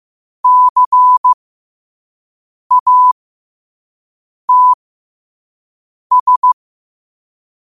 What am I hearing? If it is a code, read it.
Morse code, heard as a keyed tone, about 15 wpm, spaced as for 5 wpm, 1000 Hz. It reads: CATS
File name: qotdfiles / 5wpm